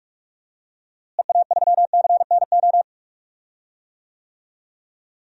Morse code, heard as a keyed tone, about 41 wpm, 700 Hz. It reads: EA3CNO
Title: DXCC-100 Call Signs - 6 Characters - Repeated 2x - 45wpm